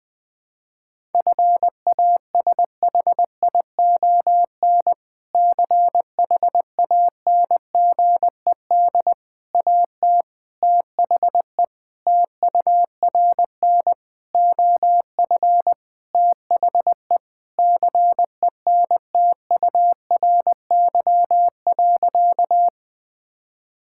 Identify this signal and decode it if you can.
Morse code, heard as a keyed tone, about 20 wpm, 700 Hz. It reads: FASHION CHANGED AT THE TURN OF THE CENTURY.
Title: Sentences with 5-Word Phrases - 20wpm